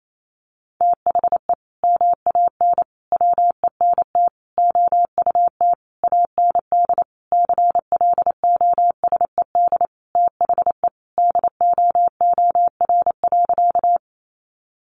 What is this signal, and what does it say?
Morse code, heard as a keyed tone, about 28 wpm, 700 Hz. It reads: THE MAN WENT OUT AND CLOSED THE DOOR.
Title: Sentences from Top 1400 Words 28wpm